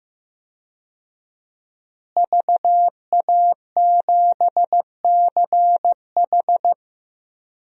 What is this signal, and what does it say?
Morse code, heard as a keyed tone, about 15 wpm, 700 Hz. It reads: VA7CH
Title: DXCC-100 Call Signs - 5 Characters - Repeated 2x - 15wpm